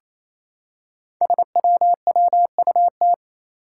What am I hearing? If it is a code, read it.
Morse code, heard as a keyed tone, about 29 wpm, 700 Hz. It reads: SWWUT
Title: Five Challenging Letter-Number Combinations 28wpm